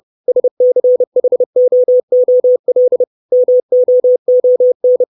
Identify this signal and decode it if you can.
Morse code, heard as a keyed tone, about 30 wpm, 500 Hz. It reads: SCHOOL MOON
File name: 632 Words 30WPM